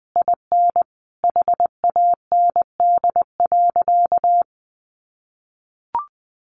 Morse code, heard as a keyed tone, about 20 wpm, 700 Hz. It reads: IN HAND.
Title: Sentences from Top 1200 Words 20wpm